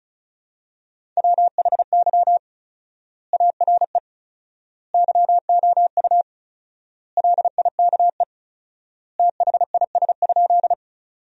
Morse code, heard as a keyed tone, about 34 wpm, 700 Hz. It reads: WHY ARE YOU LIKE THIS?
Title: 4x Word-Spacing - Sentences from Top 200 Words - 35wpm